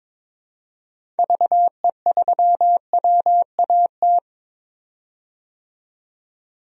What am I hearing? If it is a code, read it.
Morse code, heard as a keyed tone, about 22 wpm, 700 Hz. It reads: VE3WAT
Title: DXCC-100 Call Signs - 6 Characters - 22wpm